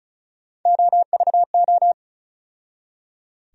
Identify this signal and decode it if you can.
Morse code, heard as a keyed tone, about 34 wpm, 700 Hz. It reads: OVO
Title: USA Call Sign Postfixes - 3 Letters 35wpm